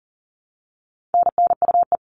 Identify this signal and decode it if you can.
Morse code, heard as a keyed tone, about 40 wpm, 700 Hz. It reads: NNUE
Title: All Cut Numbers - 4-Digit 40wpm